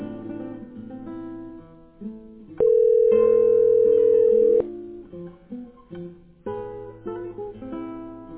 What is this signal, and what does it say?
Ringback tone (North America), 440 + 480 Hz.